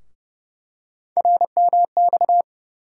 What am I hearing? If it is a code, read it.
Morse code, heard as a keyed tone, about 29 wpm, 700 Hz. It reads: RMX
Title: USA Call Sign Postfixes - 3 Letters 30wpm